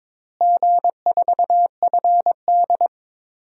Morse code, heard as a keyed tone, about 22 wpm, 700 Hz. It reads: G4FD
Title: DXCC-100 Call Signs - 4 Characters - Repeated 3x - 22wpm